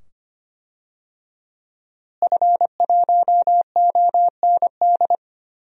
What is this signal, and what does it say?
Morse code, heard as a keyed tone, about 25 wpm, 700 Hz. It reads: F1OND
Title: DXCC-100 Call Signs - 5 Characters - Repeated 2x - 25wpm